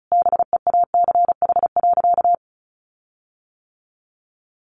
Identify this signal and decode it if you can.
Morse code, heard as a keyed tone, about 34 wpm, 700 Hz. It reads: BEACH.
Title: Sentences from Top 1500 Words 35wpm